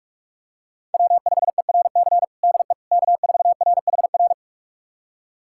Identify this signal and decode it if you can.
Morse code, heard as a keyed tone, about 46 wpm, 700 Hz. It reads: W5ERC DE K4RHR
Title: USA Callsign DE Callsign 45wpm